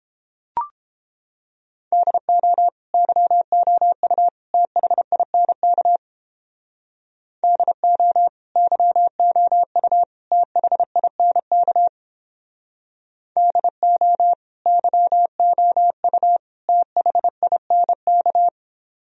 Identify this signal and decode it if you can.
Morse code, heard as a keyed tone, about 31 wpm, 700 Hz. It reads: DO YOU THINK DO YOU THINK DO YOU THINK